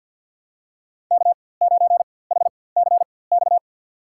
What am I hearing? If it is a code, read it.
Morse code, heard as a keyed tone, about 48 wpm, spaced as for 23 wpm, 700 Hz. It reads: K9HCX
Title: USA Call Signs - 5 Characters - 4x Character Spacing - 50wpm